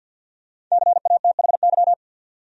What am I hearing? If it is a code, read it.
Morse code, heard as a keyed tone, about 47 wpm, 700 Hz. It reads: CATHY
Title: USA Common Names - Max 5 Letters - 50wpm